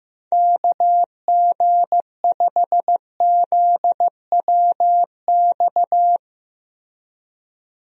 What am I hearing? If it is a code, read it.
Morse code, heard as a keyed tone, about 15 wpm, 700 Hz. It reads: KG5ZWX
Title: USA Call Signs - 6 Characters - Repeated 2x - 15wpm